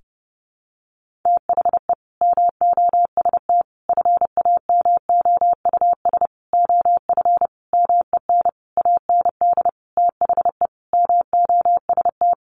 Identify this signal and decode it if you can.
Morse code, heard as a keyed tone, about 30 wpm, 700 Hz. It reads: THE MOST FAMOUS OF MEN AND THE MOST
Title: Sentences from Top 1500 Words 30wpm